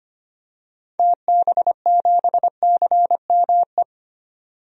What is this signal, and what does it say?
Morse code, heard as a keyed tone, about 25 wpm, 690 Hz. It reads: TB7CME
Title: DXCC-100 Call Signs - 6 Characters - 25wpm